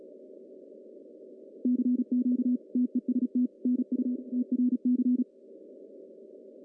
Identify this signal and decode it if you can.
Morse code, heard as a keyed tone, about 36 wpm, 250 Hz. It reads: CQ TEST NU TRC
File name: CW / cqww-cw-k5tr